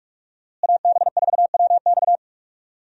Morse code, heard as a keyed tone, about 46 wpm, 700 Hz. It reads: AB4WX
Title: USA Call Signs - 5 Characters - Repeated 3x - 45wpm